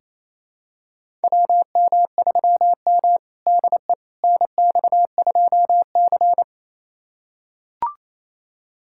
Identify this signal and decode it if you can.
Morse code, heard as a keyed tone, about 28 wpm, 700 Hz. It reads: WM3M DE NX2C
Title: USA Callsign DE Callsign - 4 Characters - 28wpm